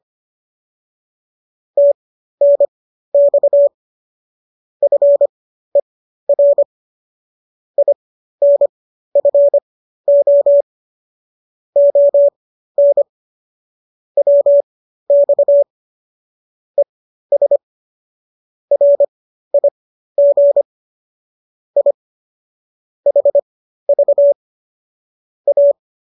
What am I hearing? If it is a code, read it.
Morse code, heard as a keyed tone, about 25 wpm, spaced as for 13 wpm, 570 Hz. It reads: TNX FER INFO ON WX ES RIG I HV A